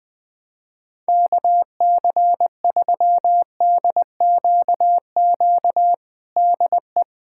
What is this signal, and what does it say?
Morse code, heard as a keyed tone, about 20 wpm, 700 Hz. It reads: KC3DQQ DE
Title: USA Callsign DE Callsign 20wpm